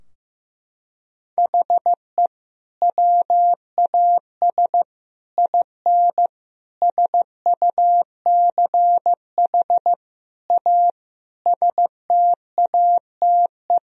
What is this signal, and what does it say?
Morse code, heard as a keyed tone, about 15 wpm, 700 Hz. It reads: HE WAS IN SUCH A STATE
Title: Sentences from Top 1000 Words - Review Part 2 - 15wpm